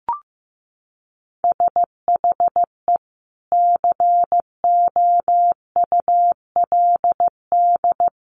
Morse code, heard as a keyed tone, about 15 wpm, 700 Hz. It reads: SHE COULD
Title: Sentences from Top 1400 Words 15wpm